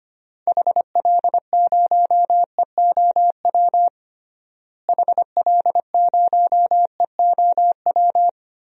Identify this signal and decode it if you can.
Morse code, heard as a keyed tone, about 25 wpm, 700 Hz. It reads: HL0EOW HL0EOW
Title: DXCC-100 Call Signs - 6 Characters - Repeated 2x - 25wpm